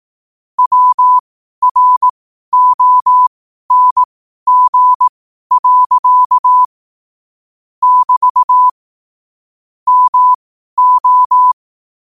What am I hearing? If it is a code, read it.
Morse code, heard as a keyed tone, about 18 wpm, spaced as for 12 wpm, 1000 Hz. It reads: WRONG. <BT> MO
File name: qotdfiles / 12wpm